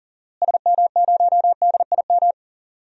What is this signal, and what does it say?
Morse code, heard as a keyed tone, about 40 wpm, 700 Hz. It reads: SM0DIM
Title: DXCC-100 Call Signs - 6 Characters - Repeated 2x - 40wpm